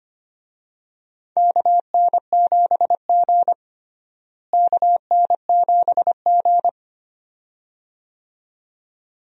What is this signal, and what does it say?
Morse code, heard as a keyed tone, about 25 wpm, 700 Hz. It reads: KN7G KN7G
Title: USA Call Signs - 4 Characters - Repeated 2x - 25wpm